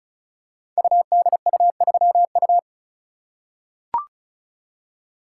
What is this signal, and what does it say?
Morse code, heard as a keyed tone, about 36 wpm, 700 Hz. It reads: UDU3U
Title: Code Groups - 5 Letter:Numbers - 35wpm